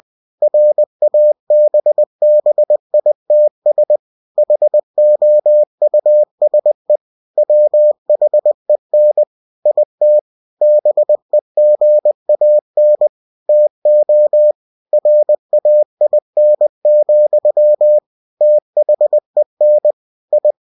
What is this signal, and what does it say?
Morse code, heard as a keyed tone, about 20 wpm, 600 Hz. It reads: RABBITS HOUSE WHEN IT BEGAN TO RAIN, THEN I
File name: cw-books / winnie-the-pooh / winnie-20wpm-page-0090